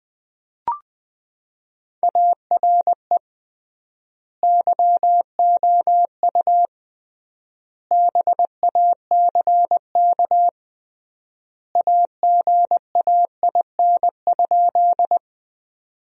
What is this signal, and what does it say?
Morse code, heard as a keyed tone, about 21 wpm, 700 Hz. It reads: ARE YOU BACK AGAIN?
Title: 3x Word-Spacing - Sentences from Top 200 Words - 20wpm